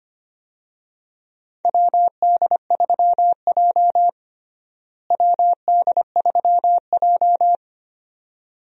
Morse code, heard as a keyed tone, about 25 wpm, 700 Hz. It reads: WD3J WD3J
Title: USA Call Signs - 4 Characters - Repeated 2x - 25wpm